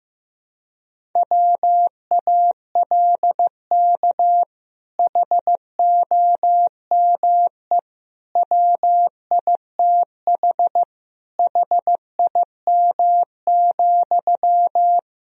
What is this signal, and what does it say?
Morse code, heard as a keyed tone, about 15 wpm, 700 Hz. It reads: WALK HOME WITH HIM,
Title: Sentences from Top 1000 Words - Review Part 1 - 15wpm